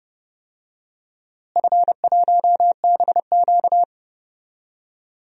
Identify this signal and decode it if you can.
Morse code, heard as a keyed tone, about 30 wpm, 700 Hz. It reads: F1BQ
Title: DXCC-100 Call Signs - 4 Characters - Repeated 2x - 30wpm